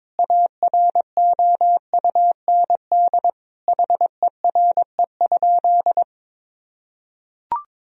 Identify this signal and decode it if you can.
Morse code, heard as a keyed tone, about 22 wpm, 700 Hz. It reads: AROUND HERE?
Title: Sentences from Top 1000 Words - Review Part 3 - 22wpm